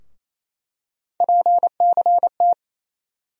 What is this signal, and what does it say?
Morse code, heard as a keyed tone, about 28 wpm, 700 Hz. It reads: PCT
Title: USA Call Sign Postfixes - 3 Letters 28wpm